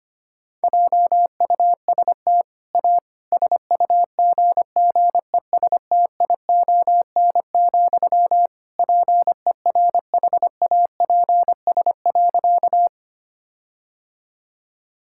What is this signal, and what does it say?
Morse code, heard as a keyed tone, about 25 wpm, 700 Hz. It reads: JUST A SUGGESTION, PERHAPS.
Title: Sentences from Top 1600 Words 25wpm